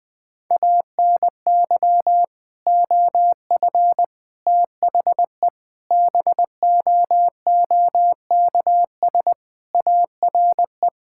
Morse code, heard as a keyed tone, about 20 wpm, 700 Hz. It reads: ANY OF THE BOOKS ARE
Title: Sentences with 3-Word Phrases - 20wpm